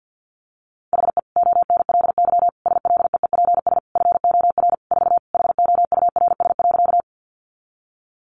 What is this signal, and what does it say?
Morse code, heard as a keyed tone, about 50 wpm, 700 Hz. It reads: HE ONLY SLEEPS FOR 4 HOURS.
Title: Additional Easy Phrases 50wpm